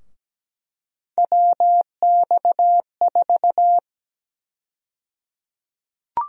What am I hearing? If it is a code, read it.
Morse code, heard as a keyed tone, about 17 wpm, 700 Hz. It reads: WX4
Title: USA Call Sign Prefixes - Repeated 2x - 17wpm